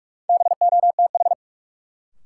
Morse code, heard as a keyed tone, about 41 wpm, 700 Hz. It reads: BOTH
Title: Sets of 4 Words - New Word Drill - 45wpm